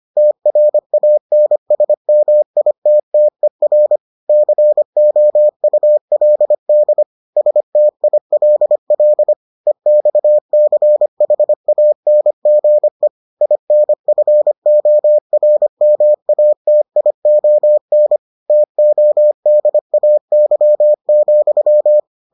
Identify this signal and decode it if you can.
Morse code, heard as a keyed tone, about 25 wpm, 600 Hz. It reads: TRANSMITTER COULD STILL EXCHANGE INFORMATION TODAY,